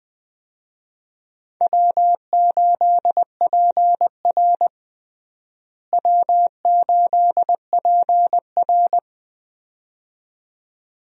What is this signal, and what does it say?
Morse code, heard as a keyed tone, about 20 wpm, 700 Hz. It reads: W8PR W8PR
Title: USA Call Signs - 4 Characters - Repeated 2x - 20wpm